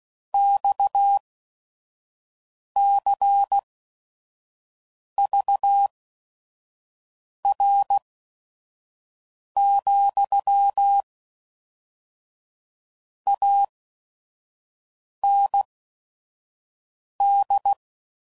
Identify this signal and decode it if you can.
Morse code, heard as a keyed tone, about 16 wpm, spaced as for 5 wpm, 800 Hz. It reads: XCVR, AND
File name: CW Code Groups @5WPM / QSOs / WD0FBT 2